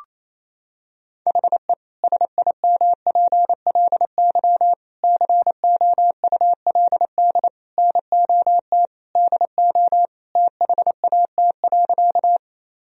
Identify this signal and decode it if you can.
Morse code, heard as a keyed tone, about 28 wpm, 700 Hz. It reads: HE SIMPLY COULD NOT DO THAT.
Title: Sentences from Top 1000 Words - Review Part 4 - 28wpm